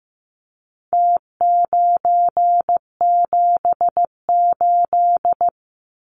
Morse code, heard as a keyed tone, about 15 wpm, 700 Hz. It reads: T978
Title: Numbers - Contest Style - 4-Digit 15wpm